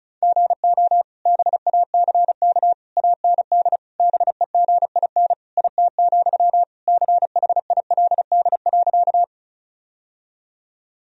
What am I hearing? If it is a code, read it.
Morse code, heard as a keyed tone, about 36 wpm, 700 Hz. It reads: GO BACK AND BEGIN IT, CHILD.
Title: Sentences from Top 1000 Words - Review Part 1 - 35wpm